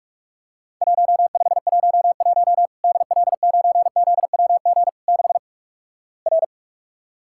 Morse code, heard as a keyed tone, about 46 wpm, 700 Hz. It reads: 1511 DL9ZWG 6 R
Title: Worked All Europe DX Contest - QTC Exchange - 45wpm